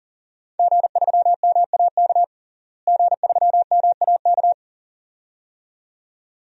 Morse code, heard as a keyed tone, about 40 wpm, 700 Hz. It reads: G3MAK G3MAK
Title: DXCC-100 Call Signs - 5 Characters - Repeated 2x - 40wpm